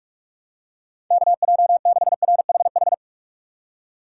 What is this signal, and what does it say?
Morse code, heard as a keyed tone, about 45 wpm, 700 Hz. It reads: KJ6RHH
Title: USA Call Signs - 6 Characters - 4x Character Spacing - 45wpm